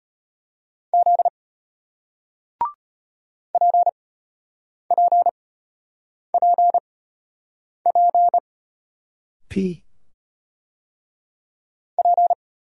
Morse code, heard as a keyed tone, about 34 wpm, spaced as for 8 wpm, 700 Hz. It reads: Z PPPP P